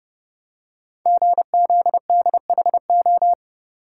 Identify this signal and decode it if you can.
Morse code, heard as a keyed tone, about 29 wpm, 700 Hz. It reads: GZDHO
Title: Code Groups - 5 Letters - 30wpm